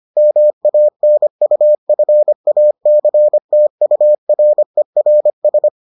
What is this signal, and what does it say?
Morse code, heard as a keyed tone, about 25 wpm, 600 Hz. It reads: MANUFACTURERS